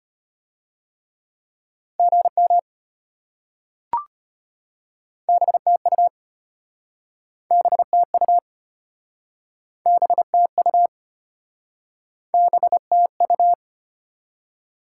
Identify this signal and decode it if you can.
Morse code, heard as a keyed tone, about 34 wpm, 700 Hz. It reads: GM BTU BTU BTU BTU